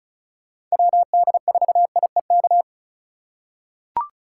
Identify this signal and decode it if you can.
Morse code, heard as a keyed tone, about 36 wpm, 700 Hz. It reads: WD4IEK
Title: USA Call Signs - 6 Characters - 2x Character Spacing - 35wpm